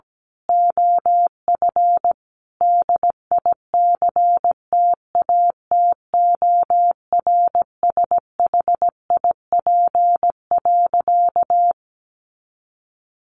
Morse code, heard as a keyed tone, about 17 wpm, 700 Hz. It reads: OF DICTATORSHIP.
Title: A Sound Of Thunder 17wpm